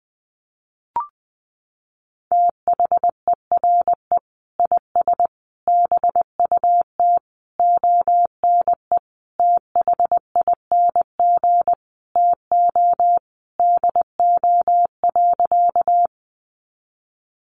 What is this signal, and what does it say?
Morse code, heard as a keyed tone, about 20 wpm, 700 Hz. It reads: THERE IS BUT ONE THING TO DO.
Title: Sentences from Top 100 Words 20wpm